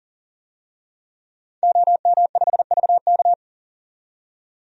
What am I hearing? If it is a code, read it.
Morse code, heard as a keyed tone, about 40 wpm, 700 Hz. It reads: OM5VK
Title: DXCC-100 Call Signs - 5 Characters - 40wpm